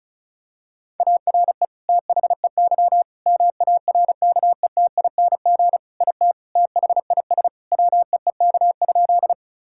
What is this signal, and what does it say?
Morse code, heard as a keyed tone, about 36 wpm, 700 Hz. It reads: ARE THEY MARKETING IT THIS WEEK?
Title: Sentences from Top 1000 Words - Review Part 5 - 35wpm